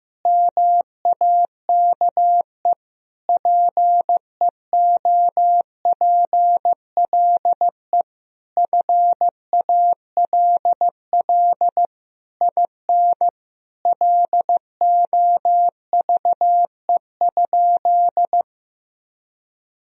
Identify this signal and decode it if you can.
Morse code, heard as a keyed tone, about 15 wpm, 700 Hz. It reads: MAKE PEOPLE FALL IN LOVE?
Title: Sentences from Top 1000 Words - Review Part 2 - 15wpm